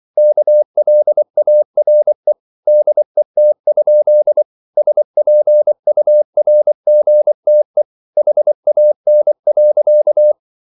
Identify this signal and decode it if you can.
Morse code, heard as a keyed tone, about 24 wpm, 600 Hz. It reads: KLARE DET? SPURGTE HAN.
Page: Ferie 23 wpm